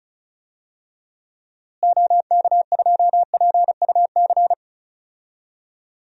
Morse code, heard as a keyed tone, about 34 wpm, 700 Hz. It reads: OK2PUC
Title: DXCC-100 Call Signs - 6 Characters - Repeated 3x - 35wpm